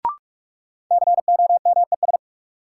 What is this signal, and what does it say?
Morse code, heard as a keyed tone, about 46 wpm, 700 Hz. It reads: COMES
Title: Sets of 3 Words - New Word Drill - 45wpm